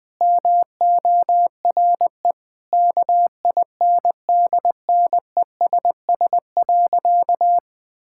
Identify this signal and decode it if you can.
Morse code, heard as a keyed tone, about 19 wpm, 700 Hz. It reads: MORE KINDNESS.
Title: Sentences with 7-Word Phrases - 20wpm